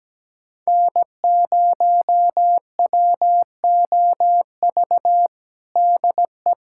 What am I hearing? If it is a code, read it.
Morse code, heard as a keyed tone, about 17 wpm, 700 Hz. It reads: N0WOV DE
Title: USA Callsign DE Callsign - 5 Characters - 17wpm